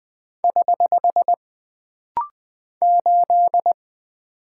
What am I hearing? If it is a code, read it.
Morse code, heard as a keyed tone, about 20 wpm, 700 Hz. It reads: <HH> 8
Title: Single Character (Letter:Number:Punctuation:Prosign) - Rapid-Fire - 20wpm